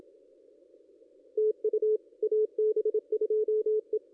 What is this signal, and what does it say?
Morse code, heard as a keyed tone, about 27 wpm, 410 Hz. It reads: TU AB2E
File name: contests / ss-cw-k5tr-op-n5rz -1900